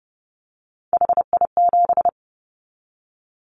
Morse code, heard as a keyed tone, about 30 wpm, 700 Hz. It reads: HI7
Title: Callsign Prefixes for Top 100 DXCC - 30wpm